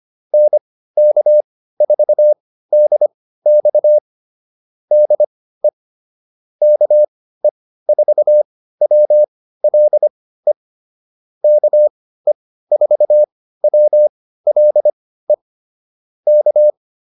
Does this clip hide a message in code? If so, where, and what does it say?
Yes: Morse code from 0.3 to 16.7 s, heard as a keyed tone, about 25 wpm, spaced as for 15 wpm, 600 Hz. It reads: NK4DX DE KE4WLE KE4WLE K